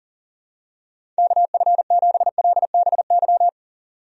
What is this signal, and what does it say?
Morse code, heard as a keyed tone, about 40 wpm, 700 Hz. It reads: KF7LBY
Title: USA Call Signs - 6 Characters - 40wpm